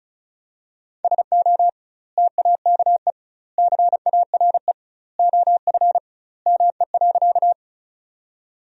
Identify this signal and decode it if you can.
Morse code, heard as a keyed tone, about 36 wpm, 700 Hz. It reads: SO TAKE CARE OF ME.
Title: 2x Word-Spacing - Sentences from Top 300 Words - 35wpm